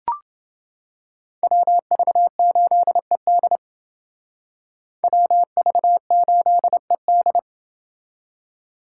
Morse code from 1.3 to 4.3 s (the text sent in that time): WV8ED